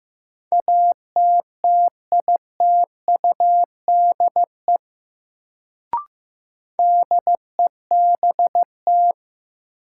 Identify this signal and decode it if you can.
Morse code, heard as a keyed tone, about 15 wpm, 700 Hz. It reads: ATTITUDE DEBT
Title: Sentences from Top 1300 Words - New Word Drill Alt - 15wpm